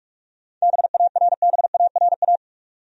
Morse code, heard as a keyed tone, about 45 wpm, 700 Hz. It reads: BARBARA
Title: USA Common Names - 45wpm